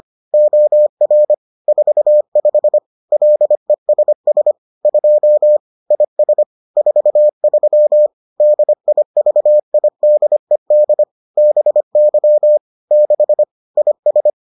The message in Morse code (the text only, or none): OR 45 LESS 2 IS 43 DIVIDED BY 6 IS